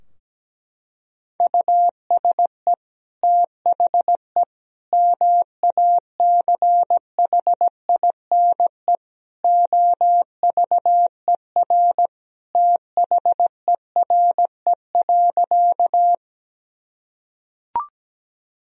Morse code, heard as a keyed tone, about 17 wpm, 700 Hz. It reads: USE THE MACHINE OVER THERE.
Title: Sentences from Top 1000 Words - Review Part 2 - 17wpm